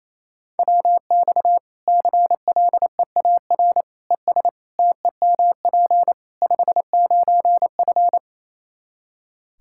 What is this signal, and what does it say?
Morse code, heard as a keyed tone, about 28 wpm, 700 Hz. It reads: WX CLEAR ES TEMP 59F
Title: K1USN SST QSO Practice 28wpm